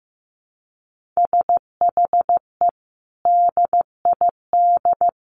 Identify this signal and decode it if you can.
Morse code, heard as a keyed tone, about 15 wpm, 700 Hz. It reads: SHE DID
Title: Sentences from Top 1500 Words 15wpm